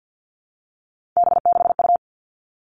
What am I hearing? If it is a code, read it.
Morse code, heard as a keyed tone, about 50 wpm, 700 Hz. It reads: B6U
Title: Three Challenging Letter-Number Combinations 50wpm